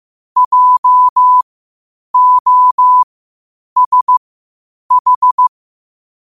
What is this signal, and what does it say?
Morse code, heard as a keyed tone, about 15 wpm, spaced as for 8 wpm, 1000 Hz. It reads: JOSH
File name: qotdfiles / 8wpm